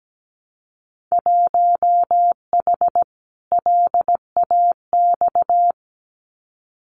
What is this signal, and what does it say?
Morse code, heard as a keyed tone, about 17 wpm, 700 Hz. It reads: 1H LAX
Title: Winter Field Day Exchanges 17wpm